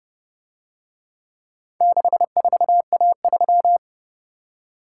Morse code, heard as a keyed tone, about 28 wpm, 700 Hz. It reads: 64A3